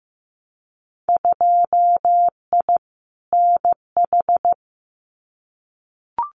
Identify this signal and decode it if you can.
Morse code, heard as a keyed tone, about 15 wpm, 700 Hz. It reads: 2I NH
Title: Winter Field Day Exchanges Encore 15wpm